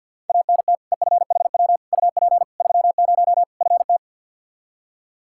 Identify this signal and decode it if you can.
Morse code, heard as a keyed tone, about 48 wpm, 700 Hz. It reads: ANT EFHW UP 30 FT
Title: Practice QSO - Exchange 2 - Info - 50wpm